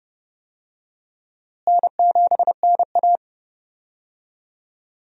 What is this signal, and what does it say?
Morse code, heard as a keyed tone, about 30 wpm, 700 Hz. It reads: N7NA
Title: USA Call Signs - 4 Characters - Repeated 3x - 30wpm